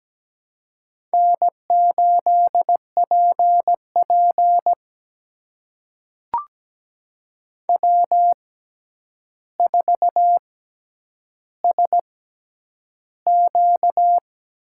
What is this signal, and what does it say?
Morse code, heard as a keyed tone, about 17 wpm, spaced as for 9 wpm, 700 Hz. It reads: N8PP W4SQ
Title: USA Call Signs - 4 Characters - 6x Character Spacing - 17wpm